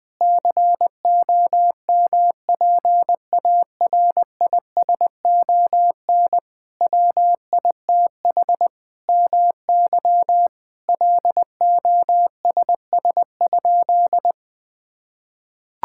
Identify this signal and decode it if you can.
Morse code, heard as a keyed tone, about 20 wpm, 700 Hz. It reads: COMPARISON WITH MY LOSS?
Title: Sentences from Top 1600 Words 20wpm